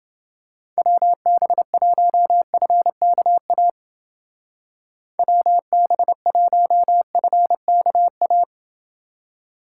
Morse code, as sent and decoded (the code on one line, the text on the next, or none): .-- -... .---- ..-. -.- .- / .-- -... .---- ..-. -.- .-
WB1FKA WB1FKA